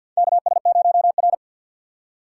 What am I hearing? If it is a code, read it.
Morse code, heard as a keyed tone, about 50 wpm, 700 Hz. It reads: KS0R
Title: USA Call Signs - 6x Character Spacing - 50wpm